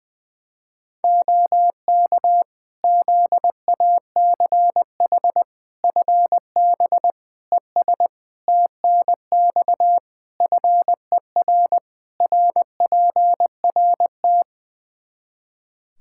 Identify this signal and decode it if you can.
Morse code, heard as a keyed tone, about 20 wpm, 700 Hz. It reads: OK ZACH FB ES TNX FER RPRT
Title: K1USN SST QSO Practice 20wpm